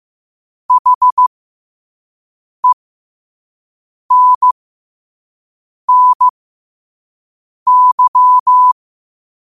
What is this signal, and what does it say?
Morse code, heard as a keyed tone, about 15 wpm, spaced as for 5 wpm, 1000 Hz. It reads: HENNY
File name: qotdfiles / 5wpm